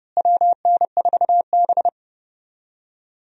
Morse code, heard as a keyed tone, about 30 wpm, 700 Hz. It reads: WN4B
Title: USA Call Signs - 4 Characters - Repeated 3x - 30wpm